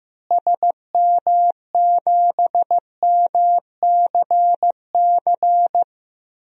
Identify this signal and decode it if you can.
Morse code, heard as a keyed tone, about 15 wpm, 700 Hz. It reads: SM7MCC
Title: DXCC-100 Call Signs - 6 Characters - Repeated 2x - 15wpm